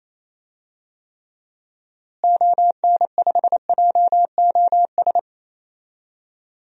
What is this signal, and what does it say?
Morse code, heard as a keyed tone, about 27 wpm, 700 Hz. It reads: ON5JOS
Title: DXCC-100 Call Signs - 6 Characters - Repeated 2x - 28wpm